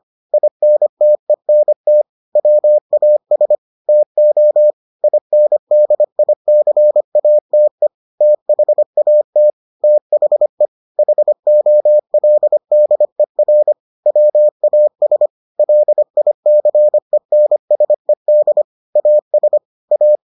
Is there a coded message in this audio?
Yes: Morse code from 0.3 to 20.2 s, heard as a keyed tone, about 25 wpm, 600 Hz. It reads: INTENT WAS TO INDICATE THAT THE HOLDER WAS LICENSED AS A